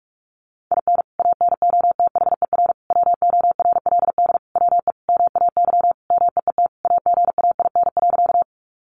Morse code, heard as a keyed tone, about 44 wpm, 700 Hz. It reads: IN ANOTHER WORLD WE MAY MEET AGAIN.
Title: Sentences from Top 1100 Words 45wpm